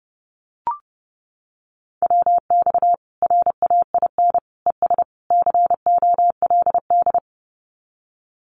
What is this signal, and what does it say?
Morse code, heard as a keyed tone, about 30 wpm, 700 Hz. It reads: WX RAIN ES COLD
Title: Common QSO Phrases 30wpm